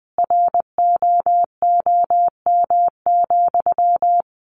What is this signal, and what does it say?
Morse code, heard as a keyed tone, about 20 wpm, 700 Hz. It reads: ROOM,
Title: Sentences from Top 600 Words 20wpm